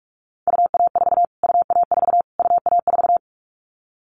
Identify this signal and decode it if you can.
Morse code, heard as a keyed tone, about 44 wpm, 700 Hz. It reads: UA4 UA4 UA4
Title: Callsign Prefixes for Top 25 DXCC - Repeated 3x - 45wpm